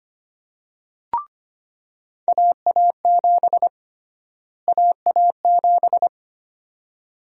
Morse code, heard as a keyed tone, about 25 wpm, 700 Hz. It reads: AA7 AA7
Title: USA Call Sign Prefixes - Repeated 2x - 25wpm